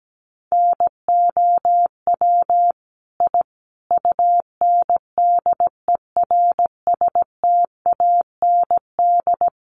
Morse code, heard as a keyed tone, about 17 wpm, 700 Hz. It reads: NOW I UNDERSTAND
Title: Sentences from Top 1000 Words 17wpm